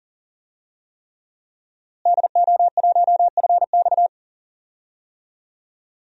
Morse code, heard as a keyed tone, about 40 wpm, 700 Hz. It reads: DO1FX
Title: DXCC-100 Call Signs - 5 Characters - Repeated 3x - 40wpm